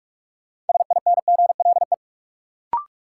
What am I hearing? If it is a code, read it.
Morse code, heard as a keyed tone, about 46 wpm, 700 Hz. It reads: SINGLE
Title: Sentences from Top 700 Words - New Word Drill - 45wpm